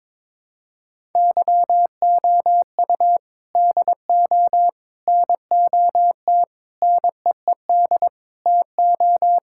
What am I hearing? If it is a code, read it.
Morse code, heard as a keyed tone, about 22 wpm, 700 Hz. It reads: YOU DO NOT NEED TO
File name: practice / 22wpm / Sentences with 4-Word Phrases - 22wpm